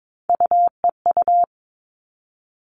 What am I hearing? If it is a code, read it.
Morse code, heard as a keyed tone, about 22 wpm, 700 Hz. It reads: UEU
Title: All Cut Numbers - 3-Digit 22wpm